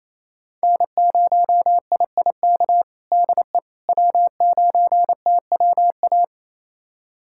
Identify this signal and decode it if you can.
Morse code, heard as a keyed tone, about 27 wpm, 700 Hz. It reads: N0IIK DE W9TWA
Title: USA Callsign DE Callsign - 5 Characters - 28wpm